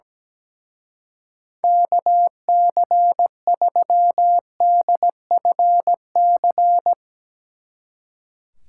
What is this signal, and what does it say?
Morse code, heard as a keyed tone, about 19 wpm, 700 Hz. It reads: KC3DFC
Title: Speed Racing - USA 2x3 Call Signs 17wpm